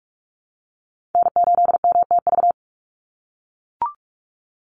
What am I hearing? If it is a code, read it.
Morse code, heard as a keyed tone, about 44 wpm, 700 Hz. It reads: N8MTV
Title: Call Signs - 45wpm